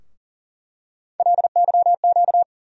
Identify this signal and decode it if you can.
Morse code, heard as a keyed tone, about 40 wpm, 700 Hz. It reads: LYQ
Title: USA Call Sign Postfixes - 3 Letters 40wpm